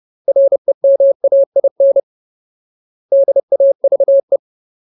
Morse code, heard as a keyed tone, about 30 wpm, 550 Hz. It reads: REMAIN DAVE
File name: Ultimate Word List / 30WPM Ultimate